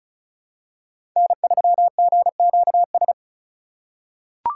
Morse code, heard as a keyed tone, about 36 wpm, 700 Hz. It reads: N3GQS
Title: USA Call Signs - 5 Characters - 4x Character Spacing - 35wpm